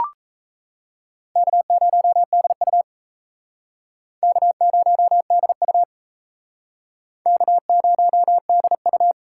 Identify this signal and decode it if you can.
Morse code, heard as a keyed tone, about 36 wpm, 700 Hz. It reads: K0DU K0DU K0DU